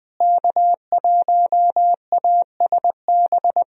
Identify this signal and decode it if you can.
Morse code, heard as a keyed tone, about 21 wpm, 700 Hz. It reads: K1ASB
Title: USA Call Signs - 5 Characters - Repeated 3x - 20wpm